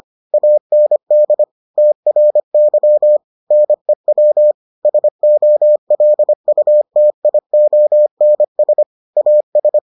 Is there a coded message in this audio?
Yes: Morse code from 0.3 to 9.8 s, heard as a keyed tone, about 25 wpm, 600 Hz. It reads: AND TRY NEW SOLUTIONS AS